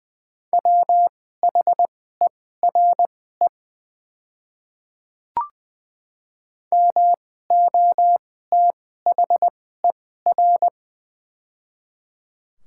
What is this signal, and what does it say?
Morse code, heard as a keyed tone, about 20 wpm, spaced as for 14 wpm, 700 Hz. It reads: WHERE MOTHER
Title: 2x Character Spacing - Top 200 Words - 20wpm